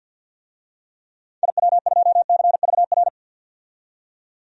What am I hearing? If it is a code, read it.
Morse code, heard as a keyed tone, about 50 wpm, 700 Hz. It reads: IW2CVR
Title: DXCC-100 Call Signs - 6 Characters - 50wpm